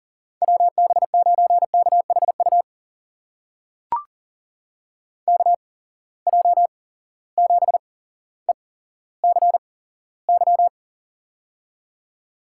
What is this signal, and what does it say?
Morse code, heard as a keyed tone, about 40 wpm, spaced as for 17 wpm, 700 Hz. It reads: WB9KHU KJ7ECY
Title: USA Call Signs - 6 Characters - 8x Character Spacing - 40wpm